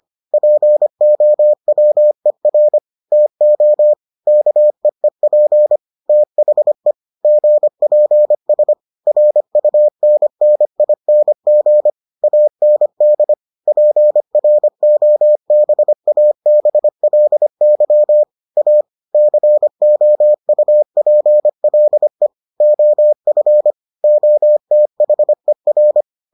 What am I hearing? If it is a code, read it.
Morse code, heard as a keyed tone, about 25 wpm, 600 Hz. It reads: POWER TO KEEP THE GPS RUNNING AND PROBABLY A COUPLE OF OTHER